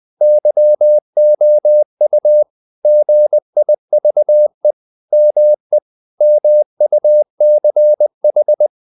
Morse code, heard as a keyed tone, about 20 wpm, 600 Hz. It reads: YOU GIVE ME MUCH
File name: the old man and the sea part 2 20wpm